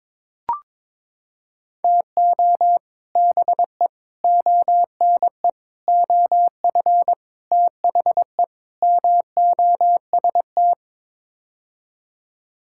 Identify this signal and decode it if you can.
Morse code, heard as a keyed tone, about 22 wpm, 700 Hz. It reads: TO BE ONE OF THE MOST
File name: practice / 22wpm / Sets of 6 Words - 22wpm